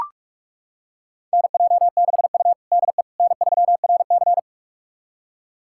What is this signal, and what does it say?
Morse code, heard as a keyed tone, about 41 wpm, 700 Hz. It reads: NJ6U DE N3RC
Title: USA Callsign DE Callsign - 4 Characters - 45wpm